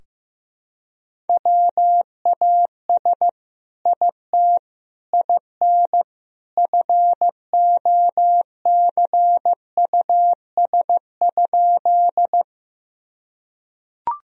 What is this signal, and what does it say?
Morse code, heard as a keyed tone, about 15 wpm, 700 Hz. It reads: WAS IT IN FOCUS?
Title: Sentences from Top 1000 Words - Review Part 4 - 15wpm